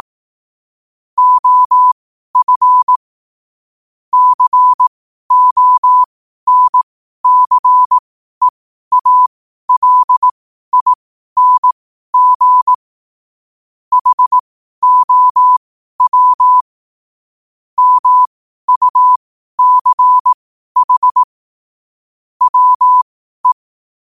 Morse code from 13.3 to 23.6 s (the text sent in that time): HOW MUCH WE